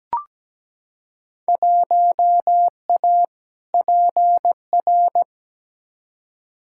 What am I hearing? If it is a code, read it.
Morse code, heard as a keyed tone, about 17 wpm, 700 Hz. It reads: 1A PR
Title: Field Day Exchanges - 17wpm